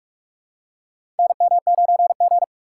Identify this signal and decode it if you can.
Morse code, heard as a keyed tone, about 44 wpm, 700 Hz. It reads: NM9G
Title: USA Call Signs - 4 Characters - 6x Character Spacing - 45wpm